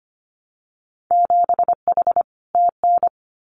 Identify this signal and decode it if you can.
Morse code, heard as a keyed tone, about 25 wpm, 700 Hz. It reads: 7H TN
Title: Winter Field Day Exchanges Encore 25wpm